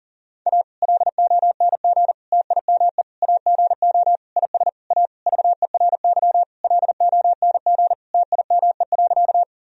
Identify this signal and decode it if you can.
Morse code, heard as a keyed tone, about 37 wpm, 700 Hz. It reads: A LONG TIME AGO IS A VERY LONG TIME.
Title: Sentences from Top 1000 Words - Review Part 3 - 40wpm